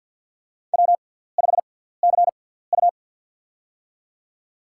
Morse code, heard as a keyed tone, about 53 wpm, spaced as for 17 wpm, 700 Hz. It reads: W5CU
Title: USA Call Signs - 4 Characters - 6x Character Spacing - 50wpm